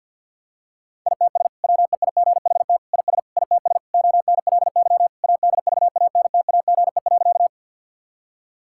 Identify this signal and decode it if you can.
Morse code, heard as a keyed tone, about 50 wpm, 700 Hz. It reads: ITS WEIGHT IS ITS ONLY ADVANTAGE.
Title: Sentences from Top 1000 Words - Review Part 6 - 50wpm